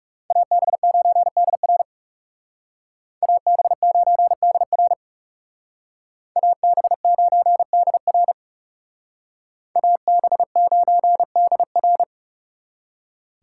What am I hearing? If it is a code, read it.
Morse code, heard as a keyed tone, about 38 wpm, 700 Hz. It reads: AB9DR AB9DR AB9DR AB9DR